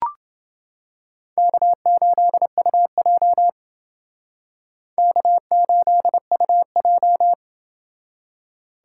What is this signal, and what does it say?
Morse code, heard as a keyed tone, about 29 wpm, 700 Hz. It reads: K8UJ K8UJ